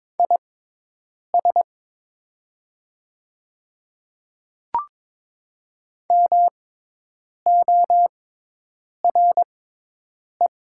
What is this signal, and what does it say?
Morse code, heard as a keyed tone, about 22 wpm, spaced as for 8 wpm, 700 Hz. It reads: IS MORE
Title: 6x Character Spacing - Top 100 Words - 22wpm